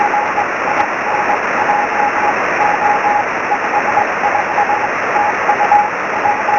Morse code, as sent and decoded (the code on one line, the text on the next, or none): none